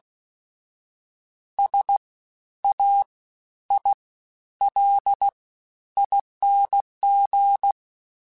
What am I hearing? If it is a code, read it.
Morse code, heard as a keyed tone, about 16 wpm, 800 Hz. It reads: S A I L ING
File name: Special INGs-mp3s / Special INGs-mp3s / SlowToFaster ing @15